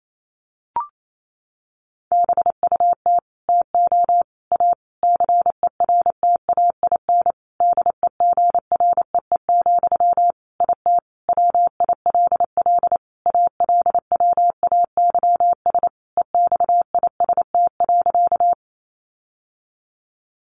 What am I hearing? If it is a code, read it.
Morse code, heard as a keyed tone, about 28 wpm, 700 Hz. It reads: BUT TO A CERTAIN DEGREE, IT WILL ALWAYS EXIST.
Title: Sentences from Top 1500 Words 28wpm